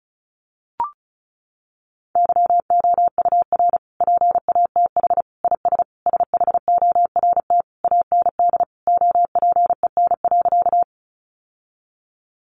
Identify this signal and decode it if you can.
Morse code, heard as a keyed tone, about 34 wpm, 700 Hz. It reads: YOUR PATH IS SHORT AND OPEN.
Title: Sentences from Top 1400 Words 35wpm